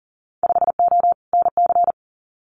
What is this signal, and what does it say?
Morse code, heard as a keyed tone, about 42 wpm, 700 Hz. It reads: 5O NC
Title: Winter Field Day Exchanges 40wpm